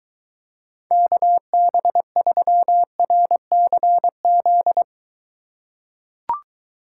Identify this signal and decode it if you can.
Morse code, heard as a keyed tone, about 23 wpm, 700 Hz. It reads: KB3RCZ